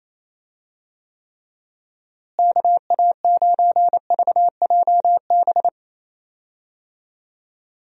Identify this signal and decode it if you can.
Morse code, heard as a keyed tone, about 28 wpm, 700 Hz. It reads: KA9VJB
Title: USA Call Signs - 6 Characters - Repeated 2x - 28wpm